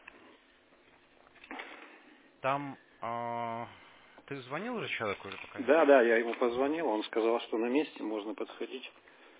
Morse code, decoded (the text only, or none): none